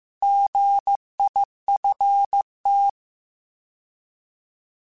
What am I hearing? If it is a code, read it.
Morse code, heard as a keyed tone, about 15 wpm, 800 Hz. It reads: GIFT